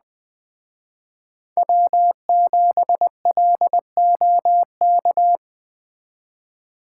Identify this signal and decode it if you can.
Morse code, heard as a keyed tone, about 20 wpm, 700 Hz. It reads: W7LOK
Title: Speed Racing - USA 1x3 Call Signs 17wpm